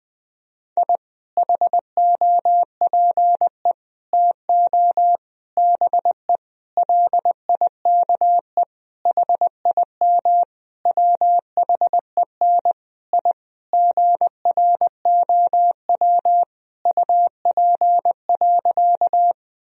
Morse code, heard as a keyed tone, about 20 wpm, 700 Hz. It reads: I HOPE TO BE LIKE HIM WHEN I GROW UP.
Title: Sentences from Top 1000 Words - Review Part 1 - 20wpm